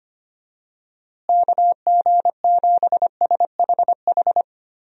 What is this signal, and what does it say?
Morse code, heard as a keyed tone, about 25 wpm, 700 Hz. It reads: KG7SHH
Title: USA Call Signs - 6 Characters - Repeated 3x - 25wpm